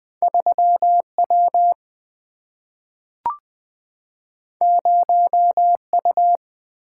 Morse code, heard as a keyed tone, about 20 wpm, 700 Hz. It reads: 3 W 0 U
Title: Single-Letter Single-Number 20wpm